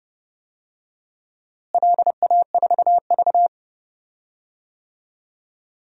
Morse code, heard as a keyed tone, about 30 wpm, 700 Hz. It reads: LA4V
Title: DXCC-100 Call Signs - 4 Characters - Repeated 2x - 30wpm